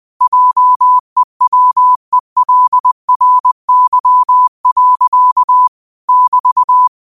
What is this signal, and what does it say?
Morse code, heard as a keyed tone, about 20 wpm, 1000 Hz. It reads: JEWELRY. <BT>
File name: qotdfiles / 20wpm